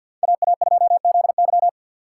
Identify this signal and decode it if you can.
Morse code, heard as a keyed tone, about 52 wpm, 700 Hz. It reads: AA2ZY
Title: USA Call Signs - 5 Characters - 3x Character Spacing - 50wpm